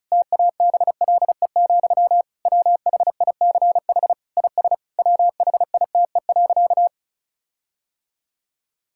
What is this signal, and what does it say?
Morse code, heard as a keyed tone, about 36 wpm, 700 Hz. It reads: TABLE, WHICH IS WHITE.
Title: Sentences from Top 1600 Words 35wpm